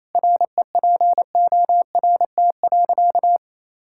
Morse code, heard as a keyed tone, about 28 wpm, 700 Hz. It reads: REPORT.
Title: Sentences from Top 1000 Words - Review Part 6 - 28wpm